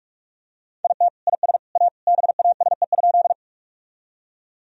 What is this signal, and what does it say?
Morse code, heard as a keyed tone, about 46 wpm, 700 Hz. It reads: IT IS A BASE?
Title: Sentences from Top 1000 Words - Review Part 1 - 45wpm